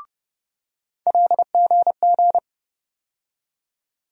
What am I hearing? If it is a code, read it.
Morse code, heard as a keyed tone, about 30 wpm, 700 Hz. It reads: LGG
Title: USA Call Sign Postfixes - 3 Letters 30wpm